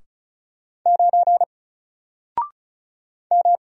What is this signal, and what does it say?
Morse code, heard as a keyed tone, about 34 wpm, 700 Hz. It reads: 9 M
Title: Single Letter-Number Alt 35wpm